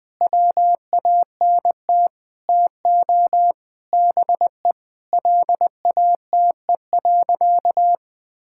Morse code, heard as a keyed tone, about 20 wpm, 700 Hz. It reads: WANT TO BE LATE.
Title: Sentences with 5-Word Phrases - 20wpm